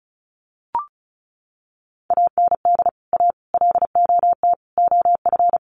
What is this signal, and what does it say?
Morse code, heard as a keyed tone, about 34 wpm, 700 Hz. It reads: AND A LOT OF
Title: Sets of 4 Words 35wpm